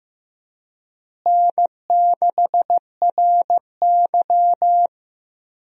Morse code, heard as a keyed tone, about 16 wpm, 700 Hz. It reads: N6RY
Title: USA Call Signs - 4 Characters - Repeated 3x - 15wpm